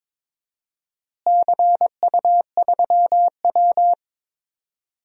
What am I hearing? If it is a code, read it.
Morse code, heard as a keyed tone, about 22 wpm, 700 Hz. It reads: CU3W
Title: DXCC-100 Call Signs - 4 Characters - Repeated 3x - 22wpm